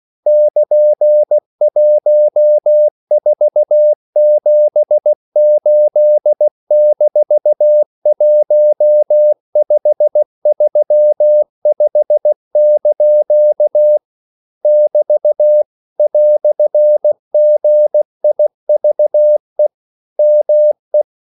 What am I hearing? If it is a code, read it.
Morse code, heard as a keyed tone, about 16 wpm, 600 Hz. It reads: <KN>1478-1535) <BT> "GIVE ME
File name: CW / Practice / EnglishQuotations0116